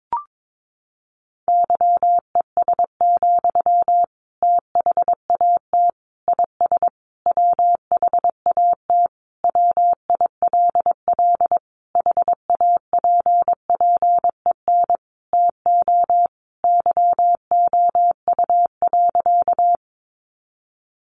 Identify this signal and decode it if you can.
Morse code, heard as a keyed tone, about 22 wpm, 700 Hz. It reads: YES, THAT IS WHAT WILL HAPPEN TO YOU.
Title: Sentences from Top 500 Words 22wpm